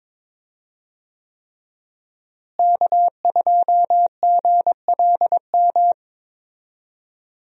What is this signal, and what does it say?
Morse code, heard as a keyed tone, about 22 wpm, 700 Hz. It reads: K2GLM
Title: USA Call Signs - 5 Characters - Repeated 3x - 22wpm